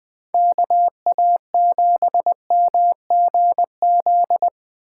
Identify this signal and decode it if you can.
Morse code, heard as a keyed tone, about 20 wpm, 700 Hz. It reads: KA7MGZ
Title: Speed Racing - USA 2x3 Call Signs 17wpm